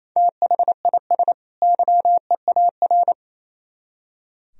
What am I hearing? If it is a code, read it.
Morse code, heard as a keyed tone, about 34 wpm, 700 Hz. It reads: THIS YEAR
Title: Speed-Racing - Sets of 2 Words 28wpm